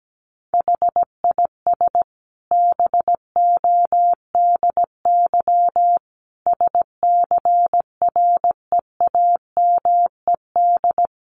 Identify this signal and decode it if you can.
Morse code, heard as a keyed tone, about 17 wpm, 700 Hz. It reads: HIS BODY SCREAMED
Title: A Sound Of Thunder 17wpm